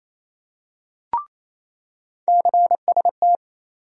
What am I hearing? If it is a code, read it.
Morse code, heard as a keyed tone, about 27 wpm, 700 Hz. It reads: CST
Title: USA Call Sign Postfixes - 3 Letters 28wpm